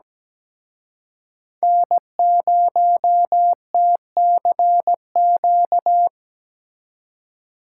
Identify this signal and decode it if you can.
Morse code, heard as a keyed tone, about 20 wpm, 700 Hz. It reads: N0TCQ
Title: Speed Racing - USA 1x3 Call Signs 17wpm